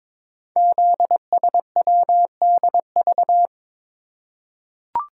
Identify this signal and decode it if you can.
Morse code, heard as a keyed tone, about 22 wpm, 700 Hz. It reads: ZSWDV
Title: Code Groups - 5 Letters - 22wpm